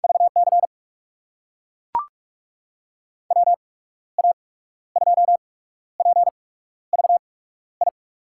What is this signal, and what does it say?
Morse code, heard as a keyed tone, about 44 wpm, spaced as for 14 wpm, 700 Hz. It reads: VC WA2PVI
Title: USA Call Signs - 6 Characters - 8x Character Spacing - 45wpm